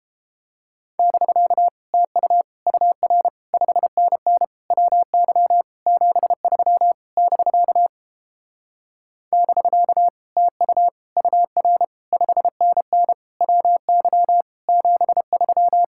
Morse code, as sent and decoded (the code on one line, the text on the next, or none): -...-.- / - ..- / ..- .-. / ..... -. -. / .-- -.-- / --... ...-- / -...-.- / -...-.- / - ..- / ..- .-. / ..... -. -. / .-- -.-- / --... ...--
<BK> TU UR 5NN WY 73 <BK> <BK> TU UR 5NN WY 73